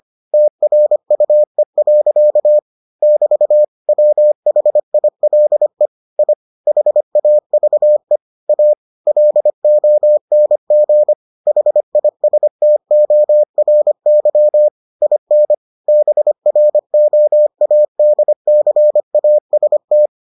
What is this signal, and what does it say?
Morse code, heard as a keyed tone, about 25 wpm, 600 Hz. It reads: TRUE. <BT> WHILE I HAVE A LONG HISTORY IN BROADCAST